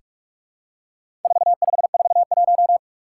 Morse code, heard as a keyed tone, about 46 wpm, 700 Hz. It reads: 4541
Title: Four Challenging Letter-Number Combinations 45wpm